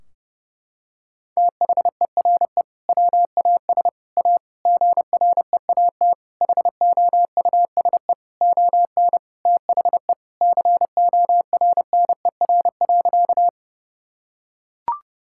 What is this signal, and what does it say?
Morse code, heard as a keyed tone, about 30 wpm, 700 Hz. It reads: THERE WAS A GREAT HOUSE ON THE CORNER.
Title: Sentences from Top 1600 Words 30wpm